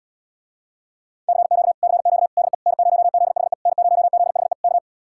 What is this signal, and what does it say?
Morse code, heard as a keyed tone, about 78 wpm, 700 Hz. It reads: CQ CQ DE N0ZHE N0ZHE K